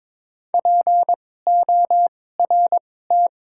Morse code, heard as a keyed tone, about 22 wpm, spaced as for 16 wpm, 700 Hz. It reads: PORT
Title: 2x Character Spacing - Top 500 Words - 22wpm